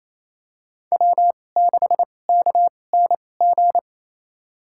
Morse code, heard as a keyed tone, about 28 wpm, spaced as for 20 wpm, 700 Hz. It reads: W6KNG
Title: USA Call Signs - 5 Characters - 2x Character Spacing - 28wpm